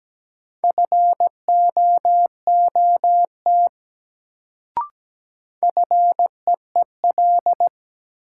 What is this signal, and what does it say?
Morse code, heard as a keyed tone, about 17 wpm, 700 Hz. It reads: FOOT FEEL
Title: Sets of 5 Words - New Word Drill - 17wpm